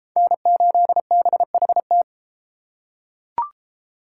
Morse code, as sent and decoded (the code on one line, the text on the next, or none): -. ---.. -... .... -
N8BHT